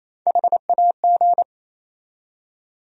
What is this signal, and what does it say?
Morse code, heard as a keyed tone, about 28 wpm, 700 Hz. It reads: HAG
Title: USA Call Sign Postfixes - 3 Letters 28wpm